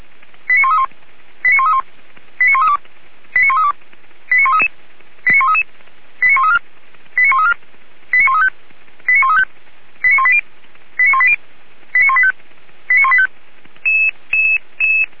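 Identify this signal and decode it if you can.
Five-tone selective call (ZVEI), heard as quick runs of five tones, tones of about 70 ms. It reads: 89131 89131 89132 89132 89133 89133 89134 89134 89135 89135 89180 89180 89185 89185
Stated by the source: ZVEI five-tone selective call 89110 received with an SDR.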